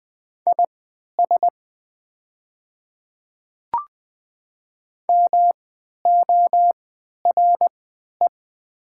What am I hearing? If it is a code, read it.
Morse code, heard as a keyed tone, about 20 wpm, spaced as for 11 wpm, 700 Hz. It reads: IS MORE
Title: 3x Character Spacing - Top 100 Words - 20wpm